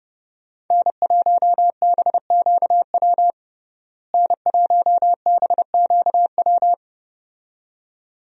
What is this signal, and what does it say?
Morse code, heard as a keyed tone, about 30 wpm, 700 Hz. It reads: N1BQW N1BQW
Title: USA Call Signs - 5 Characters - Repeated 2x - 30wpm